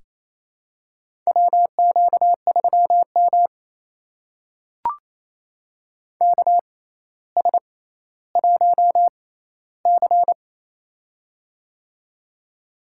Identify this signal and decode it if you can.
Morse code, heard as a keyed tone, about 28 wpm, spaced as for 14 wpm, 700 Hz. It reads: WQ3M KS1C
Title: USA Call Signs - 4 Characters - 6x Character Spacing - 28wpm